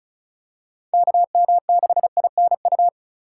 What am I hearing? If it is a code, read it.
Morse code, heard as a keyed tone, about 34 wpm, 700 Hz. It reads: KM6INU
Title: USA Call Signs - 6 Characters - 2x Character Spacing - 35wpm